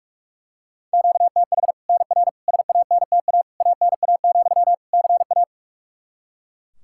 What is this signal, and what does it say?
Morse code, heard as a keyed tone, about 41 wpm, 700 Hz. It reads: QTH NR SANTA ANA, CA
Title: Practice QSO - Exchange 1 - Beginning - 45wpm